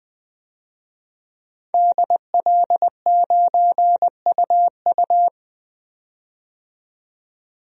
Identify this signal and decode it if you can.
Morse code, heard as a keyed tone, about 20 wpm, 700 Hz. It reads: DL9UU
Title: DXCC-100 Call Signs - 5 Characters - 20wpm